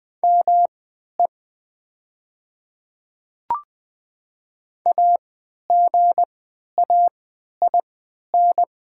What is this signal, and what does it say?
Morse code, heard as a keyed tone, about 20 wpm, spaced as for 11 wpm, 700 Hz. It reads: ME AGAIN
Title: 3x Character Spacing - Top 200 Words - 20wpm